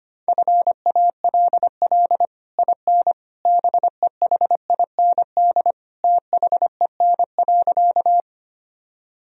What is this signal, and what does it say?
Morse code, heard as a keyed tone, about 25 wpm, 700 Hz. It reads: FALL IN BEHIND THEN.
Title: Sentences from Top 1000 Words - Review Part 2 - 25wpm